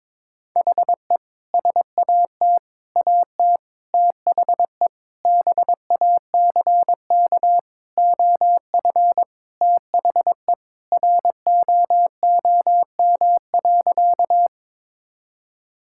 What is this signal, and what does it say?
Morse code, heard as a keyed tone, about 21 wpm, 700 Hz. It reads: HE SAT AT THE BACK OF THE ROOM.
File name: practice / 22wpm / Sentences with 3-Word Phrases - 22wpm